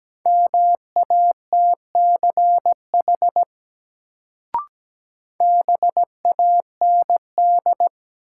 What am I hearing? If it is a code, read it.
Morse code, heard as a keyed tone, about 17 wpm, 700 Hz. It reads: MATCH BAND
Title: Sentences from Top 1600 Words - New Word Drill Alt - 17wpm